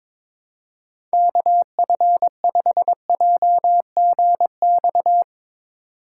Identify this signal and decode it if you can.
Morse code, heard as a keyed tone, about 22 wpm, 700 Hz. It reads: KF5JGX
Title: USA Call Signs - 6 Characters - 2x Character Spacing - 22wpm